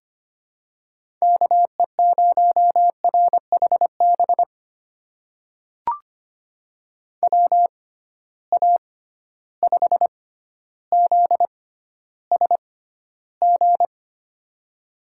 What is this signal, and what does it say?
Morse code, heard as a keyed tone, about 25 wpm, spaced as for 13 wpm, 700 Hz. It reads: KE0RHB WA5ZSG
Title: USA Call Signs - 6 Characters - 6x Character Spacing - 25wpm